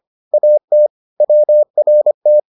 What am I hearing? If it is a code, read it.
Morse code, heard as a keyed tone, about 25 wpm, 600 Hz. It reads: AT WRT